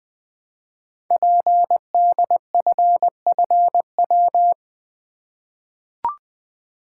Morse code, heard as a keyed tone, about 20 wpm, 700 Hz. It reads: PDFFW
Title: Code Groups - 5 Letter:Numbers - 20wpm